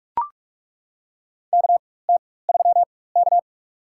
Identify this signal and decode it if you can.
Morse code, heard as a keyed tone, about 41 wpm, spaced as for 20 wpm, 700 Hz. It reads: KT3K
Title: USA Call Signs - 4 Characters - 4x Character Spacing - 45wpm